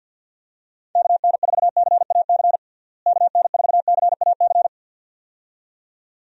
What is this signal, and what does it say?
Morse code, heard as a keyed tone, about 52 wpm, 700 Hz. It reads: KN4CAC KN4CAC
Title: USA Call Signs - 6 Characters - Repeated 2x - 50wpm